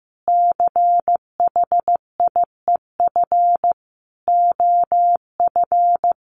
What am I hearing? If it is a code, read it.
Morse code, heard as a keyed tone, about 15 wpm, 700 Hz. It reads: CHIEF OF
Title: Sets of 5 Words Encore 15wpm